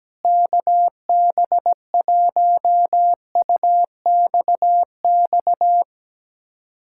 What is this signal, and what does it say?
Morse code, heard as a keyed tone, about 17 wpm, 700 Hz. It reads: KB1UXX
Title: USA Callsign DE Callsign - 6 Characters - 17wpm